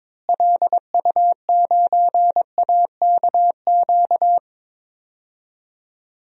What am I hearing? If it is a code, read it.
Morse code, heard as a keyed tone, about 22 wpm, 700 Hz. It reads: LU9AKQ
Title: DXCC-100 Call Signs - 6 Characters - Repeated 3x - 22wpm